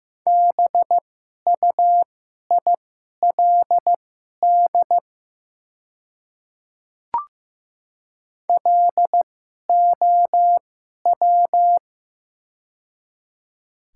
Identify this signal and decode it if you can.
Morse code, heard as a keyed tone, about 15 wpm, spaced as for 11 wpm, 700 Hz. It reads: BUILD LOW
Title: 2x Character Spacing - Top 200 Words - 15wpm